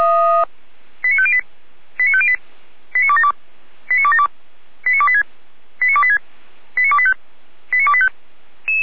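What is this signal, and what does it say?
Five-tone selective call (ZVEI), heard as quick runs of five tones, tones of about 70 ms. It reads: 89508 89508 89272 89272 89286 89286 89285 89285
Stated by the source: ZVEI five-tone call received on SDR.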